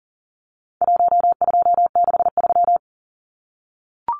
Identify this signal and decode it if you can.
Morse code, heard as a keyed tone, about 40 wpm, 700 Hz. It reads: 1263
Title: Numbers - 4-Digit 40wpm